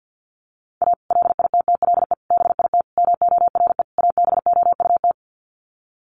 Morse code, heard as a keyed tone, about 48 wpm, 700 Hz. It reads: A LITTLE BIT MORE ABOUT
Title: Sets of 5 Words Encore2 50wpm